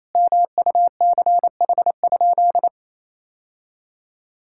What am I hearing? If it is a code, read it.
Morse code, heard as a keyed tone, about 28 wpm, 700 Hz. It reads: MUCH?
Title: Sentences from Top 1000 Words - Review Part 5 - 28wpm